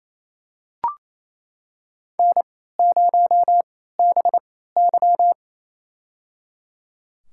Morse code, heard as a keyed tone, about 28 wpm, spaced as for 16 wpm, 700 Hz. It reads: N0BY
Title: USA Call Signs - 4 Characters - 3x Character Spacing - 28wpm